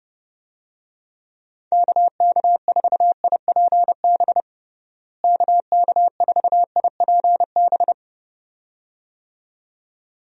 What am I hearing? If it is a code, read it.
Morse code, heard as a keyed tone, about 32 wpm, 700 Hz. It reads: KK4IPB KK4IPB
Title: USA Call Signs - 6 Characters - Repeated 2x - 30wpm